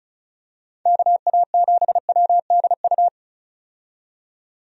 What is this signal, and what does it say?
Morse code, heard as a keyed tone, about 34 wpm, 700 Hz. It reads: KA7WDU
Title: USA Call Signs - 6 Characters - 35wpm